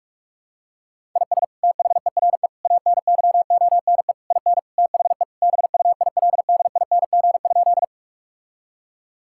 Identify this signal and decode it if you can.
Morse code, heard as a keyed tone, about 44 wpm, 700 Hz. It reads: IS THERE ANYONE IN THE BUILDING?
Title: Sentences from Top 1000 Words - Review Part 5 - 45wpm